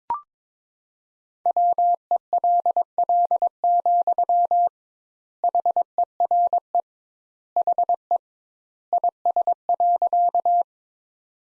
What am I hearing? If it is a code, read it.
Morse code, heard as a keyed tone, about 22 wpm, 700 Hz. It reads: WELL, HERE HE IS.
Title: Sentences Top 200 Words - 22wpm